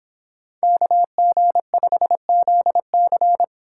Morse code, heard as a keyed tone, about 26 wpm, 700 Hz. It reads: KG5ZC